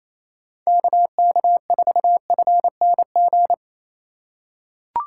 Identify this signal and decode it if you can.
Morse code, heard as a keyed tone, about 28 wpm, 700 Hz. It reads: KK4FNG
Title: USA Call Signs - 4x Character Spacing - 28wpm